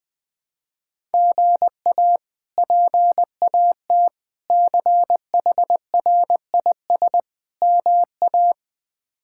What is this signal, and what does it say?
Morse code, heard as a keyed tone, about 20 wpm, 700 Hz. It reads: GA PAT CHRIS MA
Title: K1USN SST QSO Practice - 20wpm